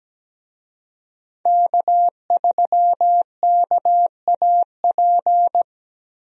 Morse code, heard as a keyed tone, about 17 wpm, 700 Hz. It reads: K3KAP
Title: USA Call Signs - 5 Characters - 17wpm